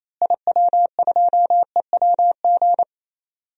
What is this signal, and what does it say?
Morse code, heard as a keyed tone, about 28 wpm, 700 Hz. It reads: IW2EWG
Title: DXCC-100 Call Signs - 6 Characters - Repeated 3x - 28wpm